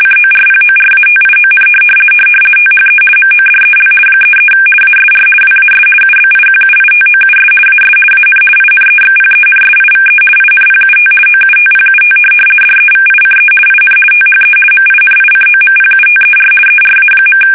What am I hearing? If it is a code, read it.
RTTY, about 50 baud. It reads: CORPORATION THE MODEL 33 ASR WAS UBIQUITOUS AS AN INEXPENSIVE INPUT OUTPUT DEVICE IN THE MINICOMPUTER ERA.
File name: fsk850-50B / teletype-corp